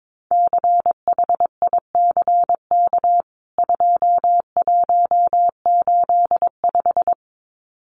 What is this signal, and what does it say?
Morse code, heard as a keyed tone, about 22 wpm, 700 Hz. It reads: CHICK 2185
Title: CWT Exchange 22wpm